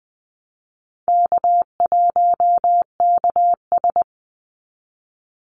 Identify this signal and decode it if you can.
Morse code, heard as a keyed tone, about 20 wpm, 700 Hz. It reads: K1KS
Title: Easy Call Signs - 20wpm